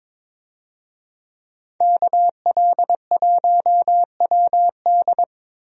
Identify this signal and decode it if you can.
Morse code, heard as a keyed tone, about 22 wpm, 700 Hz. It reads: KL1WD
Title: USA Call Signs - 5 Characters - Repeated 2x - 22wpm